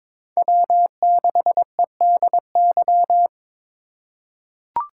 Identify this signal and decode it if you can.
Morse code, heard as a keyed tone, about 22 wpm, 700 Hz. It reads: W6EDY
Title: K1USN SST QSO Practice - 22wpm